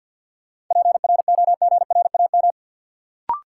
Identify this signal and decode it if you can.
Morse code, heard as a keyed tone, about 52 wpm, 700 Hz. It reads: PROGRAM
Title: Sentences from Top 1500 Words - New Word Drill Alt - 50wpm